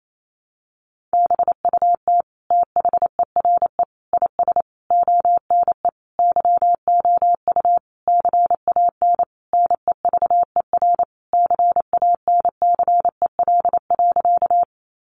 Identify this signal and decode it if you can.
Morse code, heard as a keyed tone, about 28 wpm, 700 Hz. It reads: BUT THERE IS ONE YOU CAN NEVER CANCEL.
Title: Sentences from Top 1200 Words 28wpm